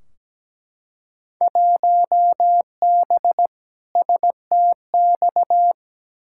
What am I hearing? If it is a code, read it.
Morse code, heard as a keyed tone, about 17 wpm, 700 Hz. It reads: 1B STX
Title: Field Day Exchanges Encore - 17wpm